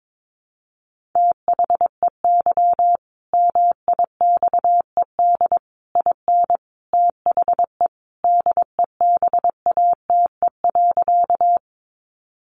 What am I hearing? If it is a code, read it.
Morse code, heard as a keyed tone, about 22 wpm, 700 Hz. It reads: THEY MIXED IN THE DEBATE.
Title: Sentences from Top 1500 Words 22wpm